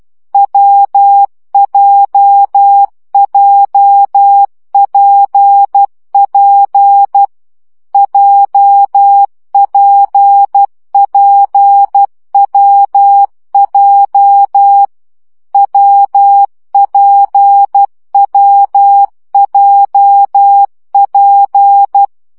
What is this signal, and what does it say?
Morse code, heard as a keyed tone, about 12 wpm, 800 Hz. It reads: WJJPP JPPWJ WPWJP
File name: CW gyakorlas w p j 10 baud